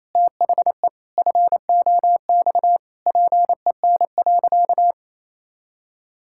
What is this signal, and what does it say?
Morse code, heard as a keyed tone, about 28 wpm, 700 Hz. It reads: THE FOX PEN.
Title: Pangram Challenge v2 - 28wpm